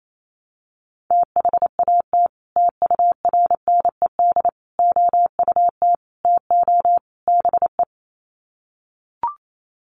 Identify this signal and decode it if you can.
Morse code, heard as a keyed tone, about 28 wpm, 700 Hz. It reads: THAT TURNED OUT TO BE
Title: Sets of 5 Words Encore2 28wpm